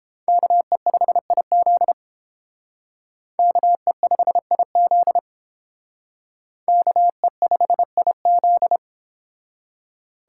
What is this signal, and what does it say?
Morse code, heard as a keyed tone, about 30 wpm, 700 Hz. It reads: KE5IZ KE5IZ KE5IZ